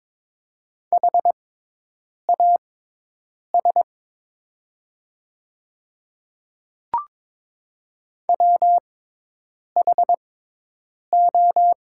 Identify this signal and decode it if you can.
Morse code, heard as a keyed tone, about 22 wpm, spaced as for 8 wpm, 700 Hz. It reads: HAS WHO
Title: 6x Character Spacing - Top 100 Words - 22wpm